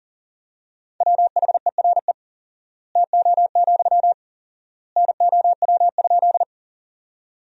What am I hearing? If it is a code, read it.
Morse code, heard as a keyed tone, about 40 wpm, 700 Hz. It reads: WHERE TO, NOW?
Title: 4x Word-Spacing - Sentences from Top 200 Words - 40wpm